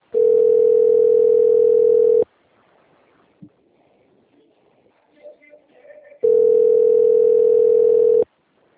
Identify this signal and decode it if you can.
Ringback tone (North America), 440 + 480 Hz.